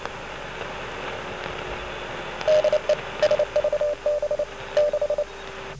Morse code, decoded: DE SVB6